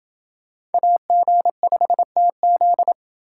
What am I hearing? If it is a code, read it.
Morse code, heard as a keyed tone, about 28 wpm, 700 Hz. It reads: AG5TZ